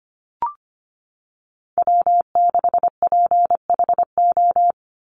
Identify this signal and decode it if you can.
Morse code, heard as a keyed tone, about 25 wpm, 700 Hz. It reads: W6PHO
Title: Call Signs - 25wpm